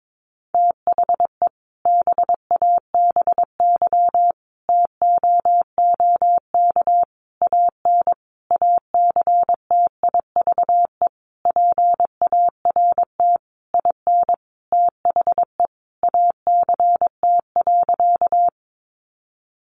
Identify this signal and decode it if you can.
Morse code, heard as a keyed tone, about 22 wpm, 700 Hz. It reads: THE BABY TOOK AN ACTIVE PART IN THE ACT.
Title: Sentences from Top 1300 Words 22wpm